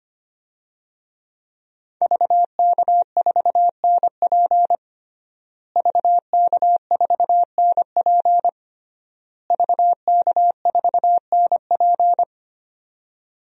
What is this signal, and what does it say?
Morse code, heard as a keyed tone, about 25 wpm, 700 Hz. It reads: VK4NP VK4NP VK4NP
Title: DXCC-100 Call Signs - 5 Characters - Repeated 3x - 25wpm